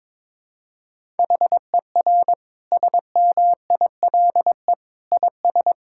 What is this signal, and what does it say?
Morse code, heard as a keyed tone, about 22 wpm, 700 Hz. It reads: HER SMILE IS
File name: Difficult Sentences with Hints - 22wpm